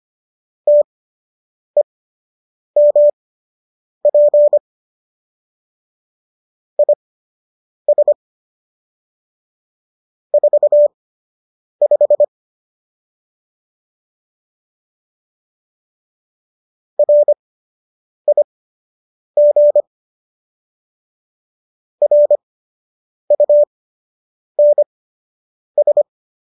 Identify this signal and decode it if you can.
Morse code, heard as a keyed tone, about 25 wpm, spaced as for 8 wpm, 590 Hz. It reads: TEMP IS 45 RIG RUNS